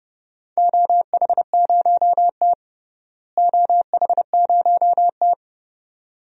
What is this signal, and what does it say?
Morse code, heard as a keyed tone, about 30 wpm, 700 Hz. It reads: OH0T OH0T
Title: DXCC-100 Call Signs - 4 Characters - Repeated 2x - 30wpm